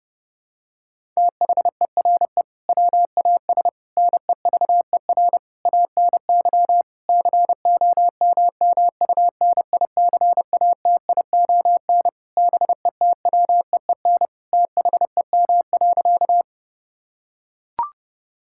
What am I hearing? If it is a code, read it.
Morse code, heard as a keyed tone, about 30 wpm, 700 Hz. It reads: THERE WAS NEVER ANY COMMUNICATION BETWEEN THEM.
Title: Sentences from Top 1000 Words - Review Part 6 - 30wpm